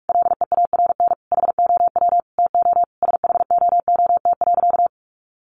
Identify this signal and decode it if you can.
Morse code, heard as a keyed tone, about 44 wpm, 700 Hz. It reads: LEARN HOW TO SHOOT.
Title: Sentences from Top 1200 Words 45wpm